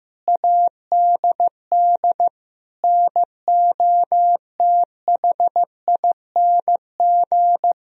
Morse code, heard as a keyed tone, about 14 wpm, 700 Hz. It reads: ADD NOTHING
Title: Sentences from Top 1000 Words - Review Part 1 - 15wpm